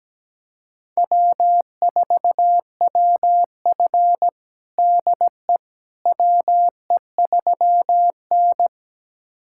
Morse code, heard as a keyed tone, about 17 wpm, 700 Hz. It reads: W4WF DE WE3N
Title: USA Callsign DE Callsign - 4 Characters - 17wpm